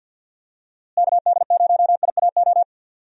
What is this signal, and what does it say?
Morse code, heard as a keyed tone, about 52 wpm, 700 Hz. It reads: KD0IAO
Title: USA Call Signs - 6 Characters - Repeated 3x - 50wpm